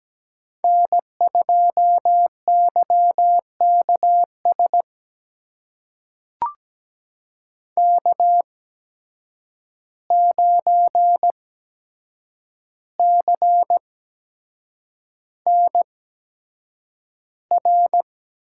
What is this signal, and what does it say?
Morse code, heard as a keyed tone, about 17 wpm, spaced as for 7 wpm, 700 Hz. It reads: N2YKS K9CNR
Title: USA Call Signs - 5 Characters - 8x Character Spacing - 17wpm